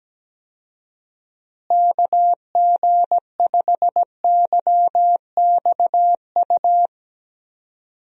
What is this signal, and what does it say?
Morse code, heard as a keyed tone, about 17 wpm, 700 Hz. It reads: KG5YXU
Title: USA Call Signs - 6 Characters - Repeated 3x - 17wpm